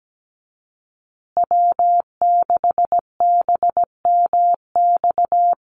Morse code, heard as a keyed tone, about 17 wpm, 700 Hz. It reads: W6BMX
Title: USA Call Signs Repeated 3x 17wpm